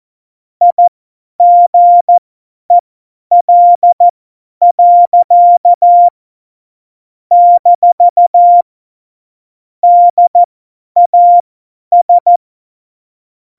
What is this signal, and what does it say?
Morse code, heard as a keyed tone, about 14 wpm, spaced as for 10 wpm, 700 Hz. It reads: IGEL. - DAS
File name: Morsekurs / Witze / Witz 28. November 2017, 50 F70